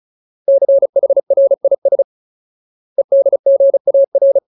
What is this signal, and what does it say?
Morse code, heard as a keyed tone, about 34 wpm, 550 Hz. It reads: CHRIS EDGAR
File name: Ultimate Word List / 35WPM Ultimate